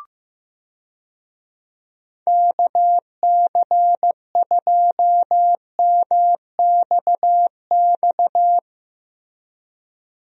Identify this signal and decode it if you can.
Morse code, heard as a keyed tone, about 15 wpm, 700 Hz. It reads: KC2MXX
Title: USA Call Signs - 6 Characters - Repeated 3x - 15wpm